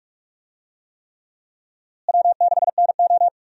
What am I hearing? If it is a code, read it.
Morse code, heard as a keyed tone, about 42 wpm, 700 Hz. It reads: W6NO
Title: USA Call Signs - 4 Characters - Repeated 3x - 45wpm